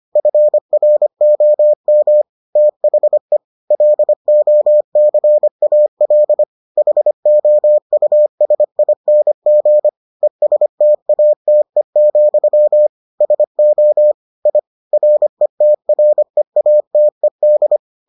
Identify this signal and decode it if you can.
Morse code, heard as a keyed tone, about 25 wpm, 600 Hz. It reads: FROM THE LOCAL HOUSING ESTATE, SO I RETREATED